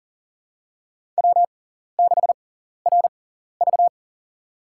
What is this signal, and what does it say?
Morse code, heard as a keyed tone, about 40 wpm, spaced as for 14 wpm, 700 Hz. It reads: W6RV
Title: USA Call Signs - 4 Characters - 6x Character Spacing - 40wpm